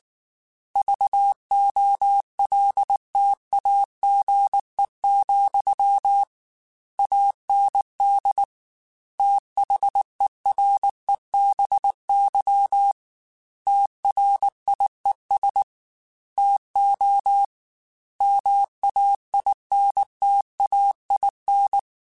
Morse code, heard as a keyed tone, about 19 wpm, 800 Hz. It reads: VOLTAGE, AND THEREBY TRIES TO MAINTAIN
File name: Technical Subjects / Technical Subjects-mp3s / XC @18